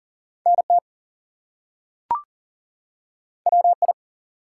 Morse code, heard as a keyed tone, about 40 wpm, 700 Hz. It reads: NT WI
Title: US States and Canadian Provinces 40wpm